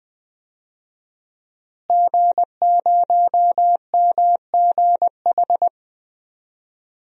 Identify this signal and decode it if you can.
Morse code, heard as a keyed tone, about 20 wpm, 700 Hz. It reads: G0MGH
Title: DXCC-100 Call Signs - 5 Characters - Repeated 2x - 20wpm